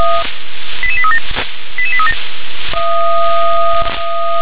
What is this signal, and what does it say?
Five-tone selective call (ZVEI), heard as quick runs of five tones, tones of about 80 ms. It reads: 88837 88837